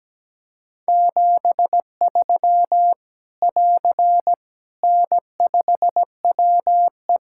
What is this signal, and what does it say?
Morse code, heard as a keyed tone, about 17 wpm, 700 Hz. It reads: 73 <AR> N5WE
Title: Practice QSO - Exchange 4 - Ending - 17wpm